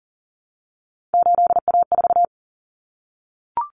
Morse code, heard as a keyed tone, about 40 wpm, 700 Hz. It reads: 8A4
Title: Common Cut Numbers - 3-Digit 40wpm